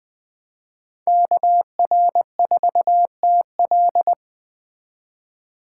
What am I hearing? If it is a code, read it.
Morse code, heard as a keyed tone, about 20 wpm, 700 Hz. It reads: KR4TL